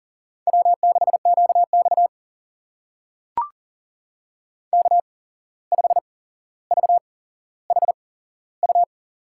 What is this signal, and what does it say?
Morse code, heard as a keyed tone, about 42 wpm, spaced as for 16 wpm, 700 Hz. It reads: W6QX K5VHU
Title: USA Call Signs - 8x Character Spacing - 40wpm